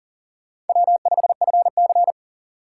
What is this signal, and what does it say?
Morse code, heard as a keyed tone, about 48 wpm, 700 Hz. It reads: W5FC
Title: Speed Racing - USA 1x2 Call Signs 40wpm